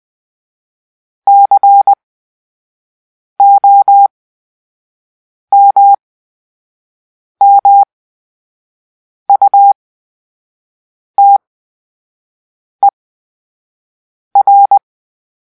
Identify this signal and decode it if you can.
Morse code, heard as a keyed tone, about 20 wpm, spaced as for 5 wpm, 800 Hz. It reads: COMMUTER
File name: Grp-3 Drill-4